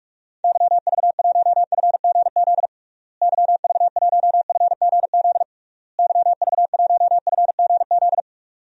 Morse code, heard as a keyed tone, about 46 wpm, 700 Hz. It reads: YV1FGZ YV1FGZ YV1FGZ
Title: DXCC-100 Call Signs - 6 Characters - Repeated 3x - 45wpm